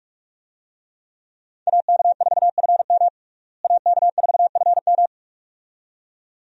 Morse code, heard as a keyed tone, about 47 wpm, 700 Hz. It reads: AK4FM AK4FM
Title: USA Call Signs - 5 Characters - Repeated 2x - 45wpm